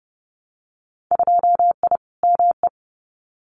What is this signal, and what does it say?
Morse code, heard as a keyed tone, about 30 wpm, 700 Hz. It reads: 2I ME
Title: Winter Field Day Exchanges Encore 30wpm